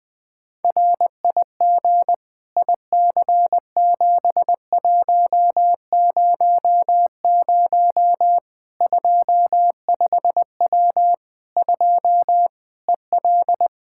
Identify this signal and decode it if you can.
Morse code, heard as a keyed tone, about 21 wpm, 700 Hz. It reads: RIG IC7100 25W 2 EL
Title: K1USN SST QSO Practice 20wpm